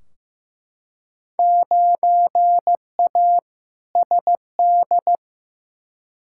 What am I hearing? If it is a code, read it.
Morse code, heard as a keyed tone, about 15 wpm, 700 Hz. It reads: 9A SD
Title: Field Day Exchanges Encore - 15wpm